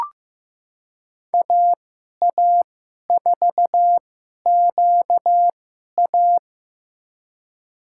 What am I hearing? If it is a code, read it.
Morse code, heard as a keyed tone, about 15 wpm, spaced as for 11 wpm, 700 Hz. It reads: AA4QA
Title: USA Call Signs - 5 Characters - 2x Character Spacing - 15wpm